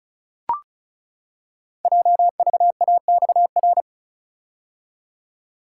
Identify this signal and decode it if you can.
Morse code, heard as a keyed tone, about 36 wpm, 700 Hz. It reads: JVAXR
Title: Code Groups - 5 Letter:Numbers - 35wpm